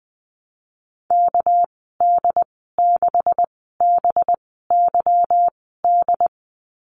Morse code, heard as a keyed tone, about 20 wpm, 700 Hz. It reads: K D 6 B Y D
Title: 200 Lesson 32g - Letter k - Call Signs 20wpm